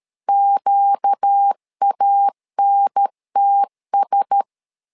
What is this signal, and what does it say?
Morse code, heard as a keyed tone, about 13 wpm, 800 Hz. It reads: QANTS
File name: LEZIONI / 5 LBVC / 5 LBVC 12wpm 1955